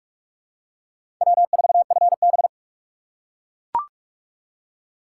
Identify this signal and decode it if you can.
Morse code, heard as a keyed tone, about 44 wpm, 700 Hz. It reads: W4FB
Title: USA Call Signs - 4 Characters - 2x Character Spacing - 45wpm